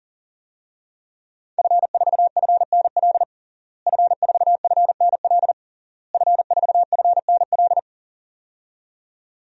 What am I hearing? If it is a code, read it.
Morse code, heard as a keyed tone, about 40 wpm, 700 Hz. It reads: F4FNL F4FNL F4FNL
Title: DXCC-100 Call Signs - 5 Characters - Repeated 3x - 40wpm